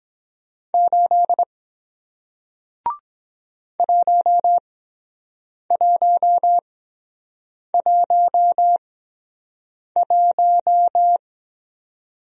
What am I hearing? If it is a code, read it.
Morse code, heard as a keyed tone, about 23 wpm, spaced as for 7 wpm, 700 Hz. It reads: 8 1111